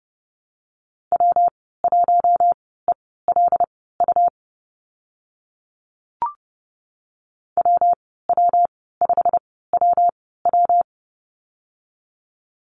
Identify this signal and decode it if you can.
Morse code, heard as a keyed tone, about 30 wpm, spaced as for 17 wpm, 700 Hz. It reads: W1ELU WW5WW
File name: Lesson 19g - Letter w - Call Signs 30wpm